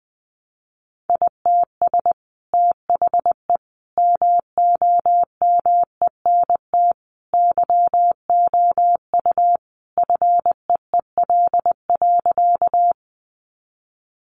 Morse code, heard as a keyed tone, about 20 wpm, 700 Hz. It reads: ITS THE MOMENT YOU FEEL.
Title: Sentences from Top 1000 Words 20wpm